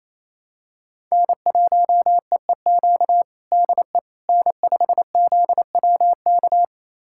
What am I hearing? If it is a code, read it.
Morse code, heard as a keyed tone, about 28 wpm, 700 Hz. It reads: N1EEQ DE N5ZWK
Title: USA Callsign DE Callsign - 5 Characters - 28wpm